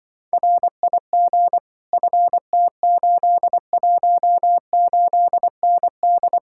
Morse code, heard as a keyed tone, about 24 wpm, 700 Hz. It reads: RIG FT818ND
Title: Speed-Racing Common QSO Phrases - 20wpm